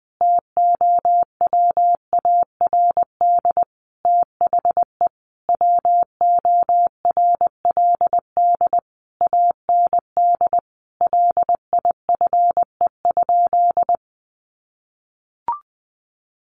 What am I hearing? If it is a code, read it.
Morse code, heard as a keyed tone, about 20 wpm, 700 Hz. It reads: TOWARD THE WORLD AND LIFE?
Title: Sentences from Top 1300 Words 20wpm